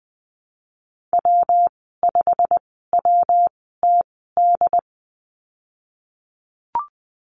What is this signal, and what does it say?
Morse code, heard as a keyed tone, about 20 wpm, spaced as for 15 wpm, 700 Hz. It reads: W5WTD
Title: 193 Lesson 31g - Letter v - Call Signs 20wpm